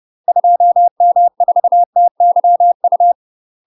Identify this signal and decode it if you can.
Morse code, heard as a keyed tone, about 30 wpm, 700 Hz. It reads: 2M4TYU
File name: UK MASTER QSOs TRAINING / 30WPM / UK MASTER QSOs 0029